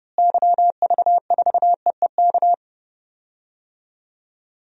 Morse code, heard as a keyed tone, about 30 wpm, 700 Hz. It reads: YV4EEK
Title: DXCC-100 Call Signs - 6 Characters - Repeated 2x - 30wpm